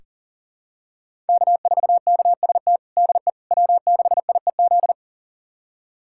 Morse code, heard as a keyed tone, about 37 wpm, 700 Hz. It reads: K4KST DE W6IEZ
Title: USA Callsign DE Callsign - 5 Characters - 40wpm